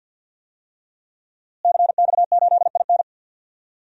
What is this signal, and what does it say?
Morse code, heard as a keyed tone, about 52 wpm, 700 Hz. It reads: CX8IN
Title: DXCC-100 Call Signs - 5 Characters - Repeated 2x - 50wpm